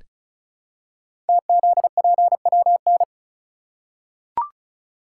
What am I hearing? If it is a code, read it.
Morse code, heard as a keyed tone, about 36 wpm, 700 Hz. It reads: TZPWN
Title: Code Groups - 5 Letter:Numbers - 35wpm